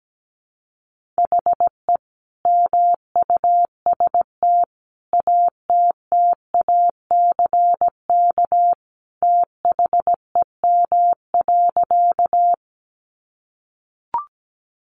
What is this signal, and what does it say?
Morse code, heard as a keyed tone, about 17 wpm, 700 Hz. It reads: HE MUST ATTACK THEM.
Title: Sentences from Top 1500 Words 17wpm